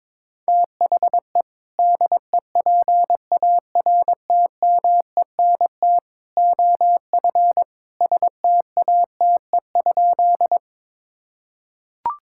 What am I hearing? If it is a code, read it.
Morse code, heard as a keyed tone, about 22 wpm, 700 Hz. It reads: THE DEPARTMENT OF STATE?
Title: Sentences from Top 1000 Words - Review Part 5 - 22wpm